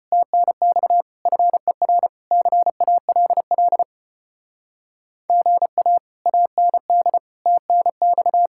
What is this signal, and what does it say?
Morse code, heard as a keyed tone, about 33 wpm, 700 Hz. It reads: TNX FER CALL GA AND TNX